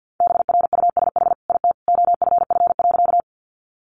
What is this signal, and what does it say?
Morse code, heard as a keyed tone, about 50 wpm, 700 Hz. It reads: BRUSH IT OFF.
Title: Sentences from Top 1100 Words 50wpm